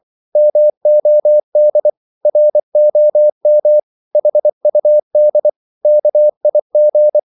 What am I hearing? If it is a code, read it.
Morse code, heard as a keyed tone, about 24 wpm, 600 Hz. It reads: MOD ROM HUD KIG
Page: Korte danske ord 24wpm